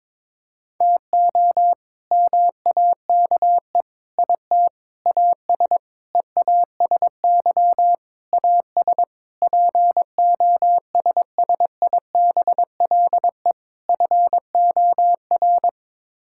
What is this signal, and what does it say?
Morse code, heard as a keyed tone, about 22 wpm, 700 Hz. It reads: TO MAKE IT AS EASY AS POSSIBLE FOR
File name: practice / 22wpm / Sets of 8 Words - 22wpm